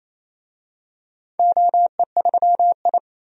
Morse code, heard as a keyed tone, about 29 wpm, 700 Hz. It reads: OE3I
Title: DXCC-100 Call Signs - 4 Characters - 28wpm